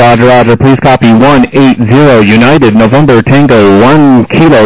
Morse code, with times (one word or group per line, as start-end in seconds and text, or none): none